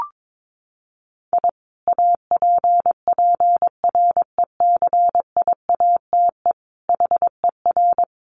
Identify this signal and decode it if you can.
Morse code, heard as a keyed tone, about 22 wpm, 700 Hz. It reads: I APPRECIATE HER
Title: Sentences from Top 1300 Words 22wpm